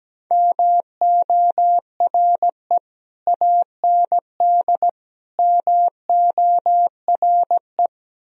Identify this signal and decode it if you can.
Morse code, heard as a keyed tone, about 17 wpm, 700 Hz. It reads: MORE AND MORE
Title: Speed-Racing - Sets of 3 Words 17wpm